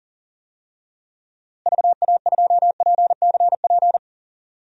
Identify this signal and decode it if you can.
Morse code, heard as a keyed tone, about 40 wpm, 700 Hz. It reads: VA2PCP
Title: DXCC-100 Call Signs - 6 Characters - Repeated 3x - 40wpm